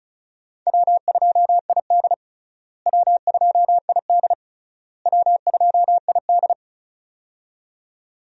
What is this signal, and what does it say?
Morse code, heard as a keyed tone, about 34 wpm, 700 Hz. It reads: W2ID W2ID W2ID
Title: USA Call Signs - 4 Characters - Repeated 3x - 35wpm